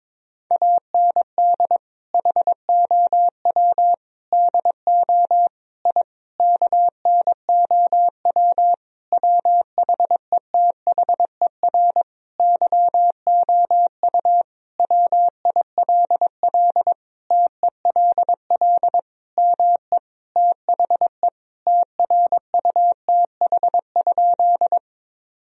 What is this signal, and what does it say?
Morse code, heard as a keyed tone, about 22 wpm, 690 Hz. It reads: AND HOW DO I KNOW WHETHER YOU WILL TELL ME THE TRUTH?
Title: Sentences from Top 1000 Words - Review Part 3 - 22wpm